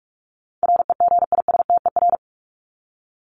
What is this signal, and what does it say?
Morse code, heard as a keyed tone, about 44 wpm, 700 Hz. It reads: REGISTER
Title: Top 1001-1250 Words 45wpm